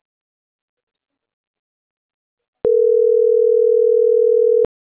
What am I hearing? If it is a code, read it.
Ringback tone (North America), 440 + 480 Hz.